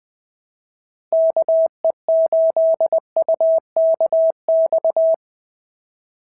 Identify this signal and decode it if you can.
Morse code, heard as a keyed tone, about 19 wpm, 650 Hz. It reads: KE8UKX
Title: POTA QSO Practice - 20wpm